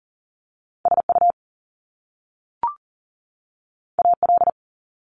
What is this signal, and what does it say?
Morse code, heard as a keyed tone, about 40 wpm, 700 Hz. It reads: SU AL
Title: Top Two Letter Combinations in Words 40wpm